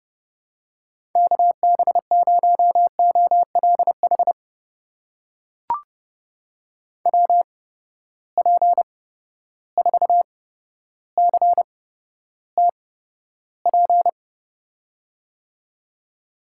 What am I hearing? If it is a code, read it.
Morse code, heard as a keyed tone, about 30 wpm, spaced as for 13 wpm, 700 Hz. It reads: KB0OLH WP4CTP
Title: USA Call Signs - 6 Characters - 8x Character Spacing - 30wpm